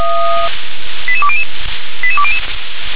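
Five-tone selective call (ZVEI), heard as quick runs of five tones, tones of about 70 ms. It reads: 88200 88200
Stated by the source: ZVEI five-tone call received on an SDR.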